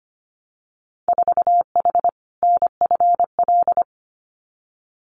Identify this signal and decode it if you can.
Morse code, heard as a keyed tone, about 25 wpm, 690 Hz. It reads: 4H NFL
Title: Winter Field Day Exchanges Encore 25wpm